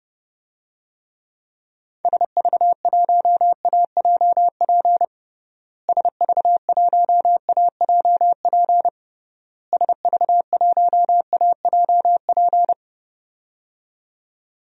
Morse code, heard as a keyed tone, about 30 wpm, 700 Hz. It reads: SV1AJP SV1AJP SV1AJP
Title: DXCC-100 Call Signs - 6 Characters - Repeated 3x - 30wpm